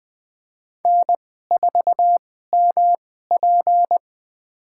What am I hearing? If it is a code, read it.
Morse code, heard as a keyed tone, about 20 wpm, spaced as for 14 wpm, 700 Hz. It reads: N4MP
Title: USA Call Signs - 4 Characters - 2x Character Spacing - 20wpm